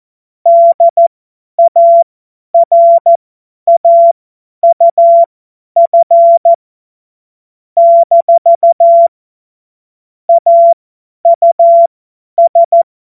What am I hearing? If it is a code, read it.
Morse code, heard as a keyed tone, about 14 wpm, spaced as for 10 wpm, 680 Hz. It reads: DARAUF - AUS
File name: Morsekurs / Witze / Witz 07. Februar 2023 50 F70